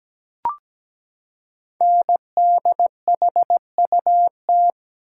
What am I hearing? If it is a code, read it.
Morse code, heard as a keyed tone, about 17 wpm, 700 Hz. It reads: NDHUT
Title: Code Groups - 5 Letters - 17wpm